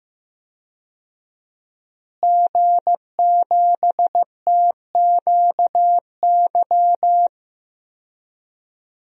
Morse code, heard as a keyed tone, about 15 wpm, 700 Hz. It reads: G7TQY
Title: DXCC-100 Call Signs - 5 Characters - Repeated 3x - 15wpm